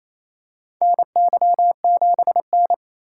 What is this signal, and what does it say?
Morse code, heard as a keyed tone, about 28 wpm, 700 Hz. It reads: NY7N
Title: K1USN SST QSO Practice - 28wpm